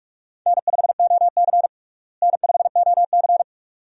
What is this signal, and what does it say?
Morse code, heard as a keyed tone, about 46 wpm, 700 Hz. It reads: N5OC N5OC
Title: USA Call Signs - 4 Characters - Repeated 2x - 45wpm